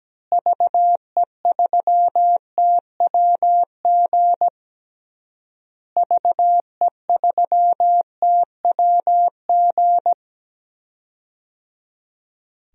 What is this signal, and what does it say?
Morse code, heard as a keyed tone, about 17 wpm, 700 Hz. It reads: VE3TWG VE3TWG
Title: DXCC-100 Call Signs - 6 Characters - Repeated 3x - 17wpm